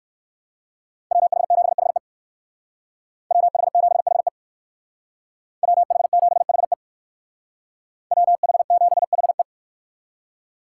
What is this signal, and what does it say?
Morse code, heard as a keyed tone, about 56 wpm, 690 Hz. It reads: WH7HE WH7HE WH7HE WH7HE
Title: Speed Racing - USA 2x2 Call Signs 45wpm